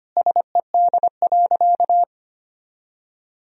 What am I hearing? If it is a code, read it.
Morse code, heard as a keyed tone, about 25 wpm, 700 Hz. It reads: SED.
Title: Sentences from Top 100 Words and Abbreviations 25wpm